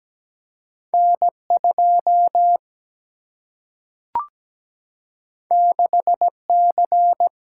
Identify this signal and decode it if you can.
Morse code, heard as a keyed tone, about 17 wpm, 700 Hz. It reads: N 2 6 C
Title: Single-Letter Single-Number 17wpm